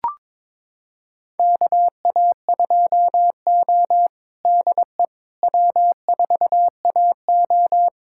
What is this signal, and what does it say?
Morse code, heard as a keyed tone, about 22 wpm, 700 Hz. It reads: KA2O DE W4AO
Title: USA Callsign DE Callsign - 4 Characters - 22wpm